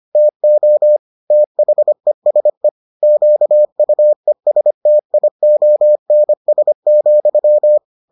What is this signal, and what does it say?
Morse code, heard as a keyed tone, about 25 wpm, 600 Hz. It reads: TO THESE QUESTIONS,